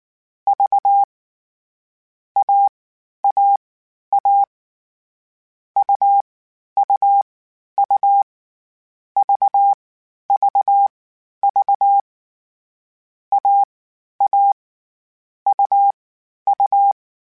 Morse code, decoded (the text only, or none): V AAA UUU VVV AA UU